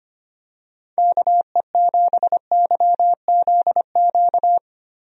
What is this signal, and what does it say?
Morse code, heard as a keyed tone, about 25 wpm, 700 Hz. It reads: KE7YZQ
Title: USA Call Signs - 6 Characters - Repeated 2x - 25wpm